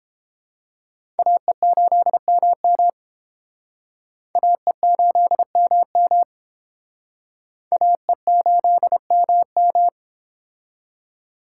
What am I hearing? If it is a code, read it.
Morse code, heard as a keyed tone, about 31 wpm, 700 Hz. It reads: AE8MM AE8MM AE8MM